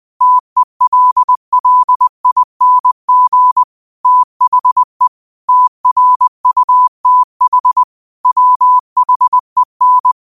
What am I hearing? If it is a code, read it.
Morse code, heard as a keyed tone, about 20 wpm, 1000 Hz. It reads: TELLING THE TRUTH WHEN
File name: qotdfiles / 20wpm